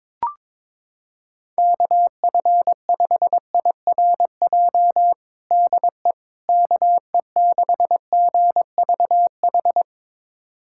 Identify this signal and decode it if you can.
Morse code, heard as a keyed tone, about 22 wpm, 700 Hz. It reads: KF5IRJ DE KE6GVH
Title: USA Callsign DE Callsign - 6 Characters - 22wpm